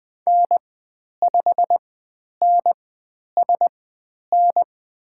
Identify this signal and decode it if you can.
Morse code, heard as a keyed tone, about 20 wpm, spaced as for 10 wpm, 700 Hz. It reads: N5NSN
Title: 109 Lesson 18g - Letter m - Call Signs 20wpm